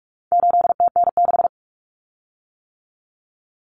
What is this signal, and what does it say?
Morse code, heard as a keyed tone, about 44 wpm, 700 Hz. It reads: 8TN6
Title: Common Cut Numbers - 4-Digit 45wpm